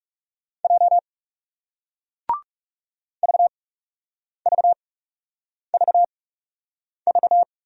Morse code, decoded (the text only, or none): J VVVV